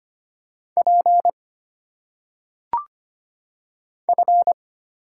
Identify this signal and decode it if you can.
Morse code, heard as a keyed tone, about 25 wpm, 700 Hz. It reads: P F
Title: Single Letters - Phonetic Pronunciation 25wpm